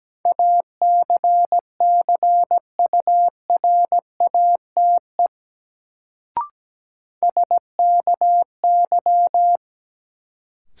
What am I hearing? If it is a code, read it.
Morse code, heard as a keyed tone, about 17 wpm, 700 Hz. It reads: ACCURATE SKY
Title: Sentences from Top 1200 Words - New Word Drill - 17wpm